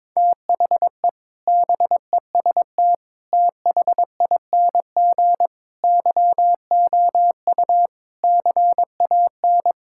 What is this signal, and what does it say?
Morse code, heard as a keyed tone, about 22 wpm, 700 Hz. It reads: THE BEST THING YOU CAN
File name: practice / 22wpm / Sets of 7 Words - 22wpm